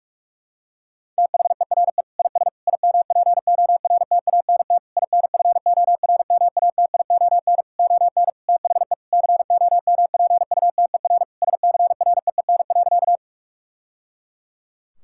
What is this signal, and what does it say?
Morse code, heard as a keyed tone, about 46 wpm, 700 Hz. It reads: THERE IS IMPORTANT INFORMATION ON THE COMPUTER SCREEN.
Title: Sentences from Top 1000 Words - Review Part 6 - 45wpm